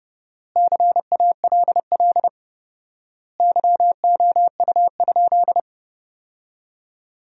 Morse code, heard as a keyed tone, about 32 wpm, 700 Hz. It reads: CALL YOU?
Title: 4x Word-Spacing - Sentences from Top 100 Words - 30wpm